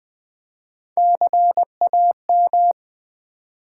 Morse code, heard as a keyed tone, about 20 wpm, 700 Hz. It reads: CAM
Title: Build A Word - 3 Letters At A Time - 6-Letter Words - 20wpm